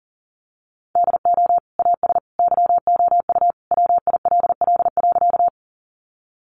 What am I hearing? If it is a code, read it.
Morse code, heard as a keyed tone, about 37 wpm, 700 Hz. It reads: DO AS YOU WILL.
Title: Sentences from Top 100 Words 40wpm